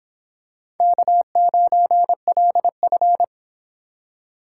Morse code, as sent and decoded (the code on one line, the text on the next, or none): -.- ----. .-.. ..-.
K9LF